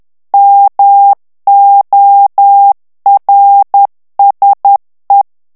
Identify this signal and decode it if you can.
Morse code, heard as a keyed tone, about 10 wpm, 800 Hz. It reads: MORSE
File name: morse mp3 / ITALIEN texte clair 1 vitesse 10 mots minute